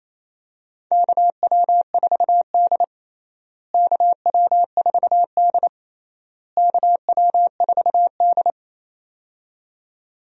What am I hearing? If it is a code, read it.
Morse code, heard as a keyed tone, about 28 wpm, 700 Hz. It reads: KW4D KW4D KW4D
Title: USA Call Signs - 4 Characters - Repeated 3x - 28wpm